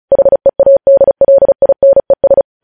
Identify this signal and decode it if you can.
Morse code, heard as a keyed tone, about 36 wpm, 550 Hz. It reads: HEADLINES
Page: news headlines in morse code at 35 wpm